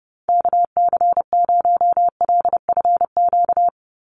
Morse code, heard as a keyed tone, about 28 wpm, 700 Hz. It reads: KC0LFQ
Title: USA Call Signs 30wpm